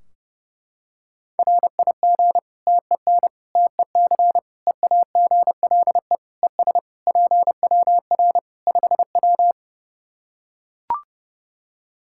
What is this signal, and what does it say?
Morse code, heard as a keyed tone, about 32 wpm, 700 Hz. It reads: RIG TEN TEC EAGLE ES PWR 5W
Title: Practice QSO - Exchange 2 - Info - 30wpm